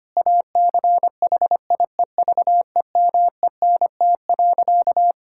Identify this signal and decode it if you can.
Morse code, heard as a keyed tone, about 25 wpm, 700 Hz. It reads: ACHIEVEMENT.
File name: practice / 25wpm / Difficult Sentences with Hints - 25wpm